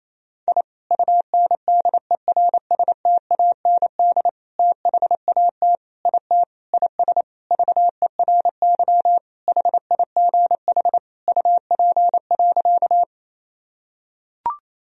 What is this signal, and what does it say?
Morse code, heard as a keyed tone, about 27 wpm, 700 Hz. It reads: I UNDERSTAND THAT IT IS VERY HIGH UP.
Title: Sentences from Top 1000 Words - Review Part 1 - 28wpm